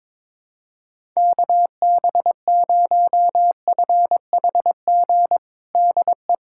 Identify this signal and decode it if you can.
Morse code, heard as a keyed tone, about 22 wpm, 700 Hz. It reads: KB0FHG DE
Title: USA Callsign DE Callsign - 6 Characters - 22wpm